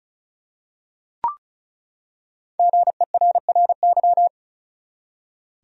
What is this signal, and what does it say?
Morse code, heard as a keyed tone, about 36 wpm, 700 Hz. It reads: GERRY
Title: USA Common Names - 35wpm